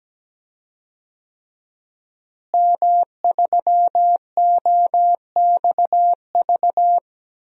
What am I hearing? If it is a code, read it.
Morse code, heard as a keyed tone, about 17 wpm, 700 Hz. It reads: M3OXV
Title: DXCC-100 Call Signs - 5 Characters - Repeated 3x - 17wpm